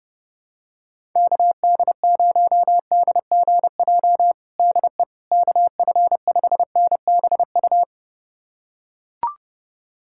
Morse code, heard as a keyed tone, about 30 wpm, 700 Hz. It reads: KD0DGJ DE KF5NBU
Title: USA Callsign DE Callsign - 6 Characters - 30wpm